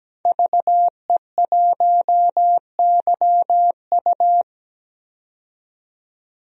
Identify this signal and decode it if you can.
Morse code, heard as a keyed tone, about 17 wpm, 700 Hz. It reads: VE1YU
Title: DXCC-100 Call Signs - 5 Characters - 17wpm